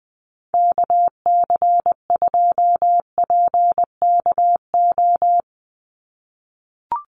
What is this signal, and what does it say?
Morse code, heard as a keyed tone, about 20 wpm, 700 Hz. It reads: KC2PKO
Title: USA Call Signs 20wpm